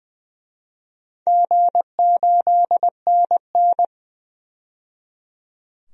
Morse code, heard as a keyed tone, about 20 wpm, 700 Hz. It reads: G8NN
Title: DXCC-100 Call Signs - 4 Characters - 20wpm